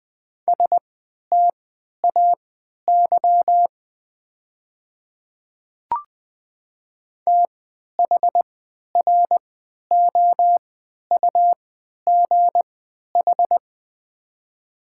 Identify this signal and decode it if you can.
Morse code, heard as a keyed tone, about 20 wpm, spaced as for 11 wpm, 700 Hz. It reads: STAY THROUGH
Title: 3x Character Spacing - Top 500 Words - 20wpm